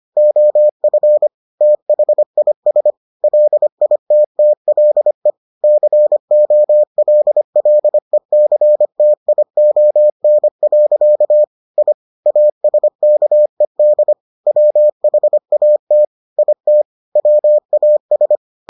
Morse code, heard as a keyed tone, about 25 wpm, 600 Hz. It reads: OF THIS LITTLE COLLECTION. I ASKED WHAT IT WAS